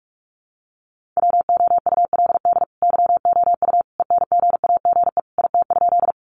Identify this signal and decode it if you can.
Morse code, heard as a keyed tone, about 44 wpm, 700 Hz. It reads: WOULD YOU ENGAGE IT?
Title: Sentences from Top 1500 Words 45wpm